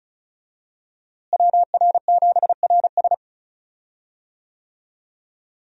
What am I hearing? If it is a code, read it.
Morse code, heard as a keyed tone, about 36 wpm, 700 Hz. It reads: WR7RS
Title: USA Call Signs - 5 Characters - Repeated 3x - 35wpm